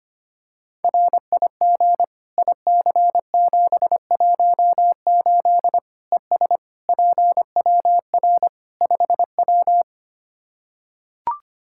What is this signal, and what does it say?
Morse code, heard as a keyed tone, about 25 wpm, 700 Hz. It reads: RIG IC718 ES PWR 5W
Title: K1USN SST QSO Practice 25wpm